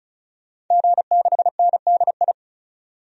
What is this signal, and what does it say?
Morse code, heard as a keyed tone, about 36 wpm, 700 Hz. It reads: G6NDI
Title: DXCC-100 Call Signs - 5 Characters - 35wpm